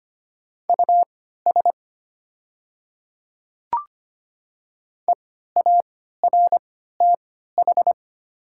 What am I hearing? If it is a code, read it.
Morse code, heard as a keyed tone, about 25 wpm, spaced as for 14 wpm, 700 Hz. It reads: US EARTH
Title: 3x Character Spacing - Top 200 Words - 25wpm